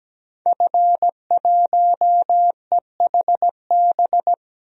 Morse code, heard as a keyed tone, about 17 wpm, 700 Hz. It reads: F1EHB
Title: DXCC-100 Call Signs - 5 Characters - 17wpm